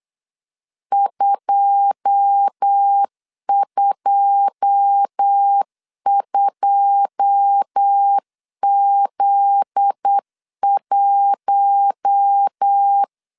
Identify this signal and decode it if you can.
Morse code, heard as a keyed tone, about 8 wpm, 800 Hz. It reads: 222Z1
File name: LEZIONI / 9 2 8wpm 3437